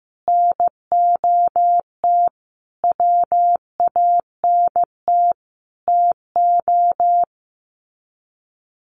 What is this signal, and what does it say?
Morse code, heard as a keyed tone, about 15 wpm, 700 Hz. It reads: NOT WANT TO
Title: Sets of 5 Words 15wpm